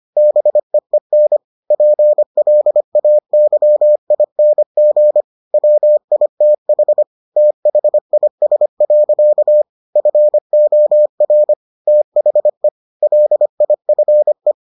Morse code, heard as a keyed tone, about 25 wpm, 600 Hz. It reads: BEEN PLAYING WITH THIS. FOR THE LIFE